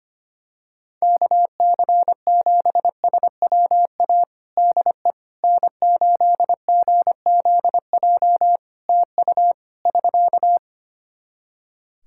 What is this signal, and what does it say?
Morse code, heard as a keyed tone, about 25 wpm, 700 Hz. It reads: KC7SWA DE N8GZJ TU <SK>
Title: Practice QSO - Exchange 4 - Ending - 25wpm